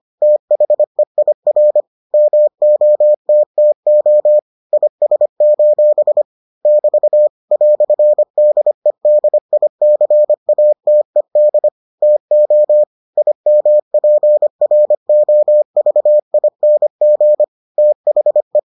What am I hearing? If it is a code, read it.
Morse code, heard as a keyed tone, about 25 wpm, 600 Hz. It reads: THEIR MOTTO IS: <BT> "DEDICATED TO IMPROVING THE